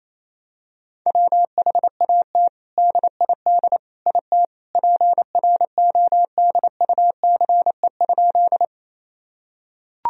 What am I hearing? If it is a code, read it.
Morse code, heard as a keyed tone, about 28 wpm, 700 Hz. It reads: WHAT DID IT PRODUCE?
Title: Sentences from Top 1000 Words - Review Part 3 - 28wpm